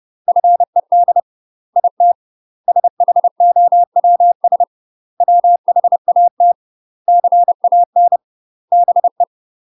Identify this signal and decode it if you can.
Morse code, heard as a keyed tone, about 30 wpm, 700 Hz. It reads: FED IT SHOWS WHAT CAN BE
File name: UK MASTER QSOs TRAINING / 30WPM / UK MASTER QSOs 0000